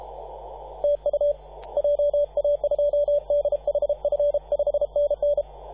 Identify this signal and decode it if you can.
Morse code, heard as a keyed tone, about 33 wpm, 580 Hz. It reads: TU JA2DHF 5NN
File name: posts / 2015-sept-24-2118z-t2gc-18-071-cw